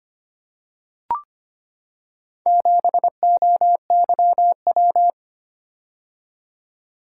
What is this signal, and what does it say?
Morse code, heard as a keyed tone, about 25 wpm, 700 Hz. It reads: 7OYW
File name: practice / 25wpm / Four Letter-Number Combinations - 25wpm